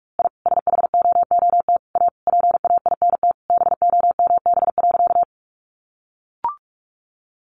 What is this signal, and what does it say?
Morse code, heard as a keyed tone, about 44 wpm, 700 Hz. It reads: I SHOOT A PAINT BOMB.
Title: A Sound Of Thunder 45wpm